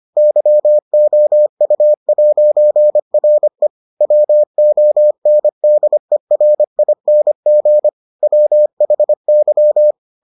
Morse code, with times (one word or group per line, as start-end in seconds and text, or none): 0.2-3.7 YOU'RE
4.0-7.9 WONDERING
8.2-9.9 WHY